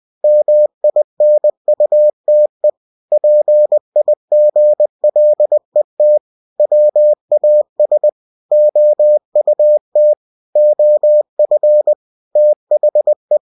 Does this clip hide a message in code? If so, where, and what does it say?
Yes: Morse code from 0.2 to 13.4 s, heard as a keyed tone, about 20 wpm, 600 Hz. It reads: MINUTE PIGLET WAS OUT OF THE